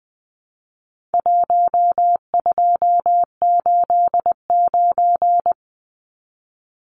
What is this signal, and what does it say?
Morse code, heard as a keyed tone, about 20 wpm, 700 Hz. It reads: 1289
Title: Numbers - Contest Style - 4-Digit 20wpm